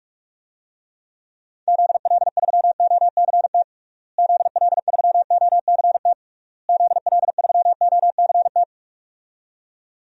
Morse code, heard as a keyed tone, about 42 wpm, 700 Hz. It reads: ZL3OCT ZL3OCT ZL3OCT
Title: DXCC-100 Call Signs - 6 Characters - Repeated 3x - 45wpm